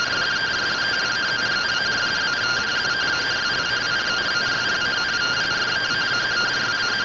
Two-tone teleprinter signal, about 170 Hz shift.
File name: findings / CORONAL fr69 nb202 3mm.gen-30 rtty rx